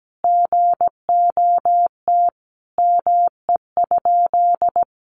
Morse code, heard as a keyed tone, about 17 wpm, 700 Hz. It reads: GOT ME?
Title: Sentences from Top 300 Words 17wpm